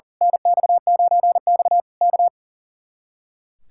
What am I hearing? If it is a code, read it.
Morse code, heard as a keyed tone, about 38 wpm, 700 Hz. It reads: NX9X K
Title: Speed-Racing Common QSO Phrases - 40wpm